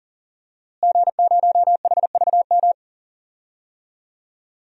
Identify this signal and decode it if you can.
Morse code, heard as a keyed tone, about 40 wpm, 700 Hz. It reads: G0HVM
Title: DXCC-100 Call Signs - 5 Characters - Repeated 3x - 40wpm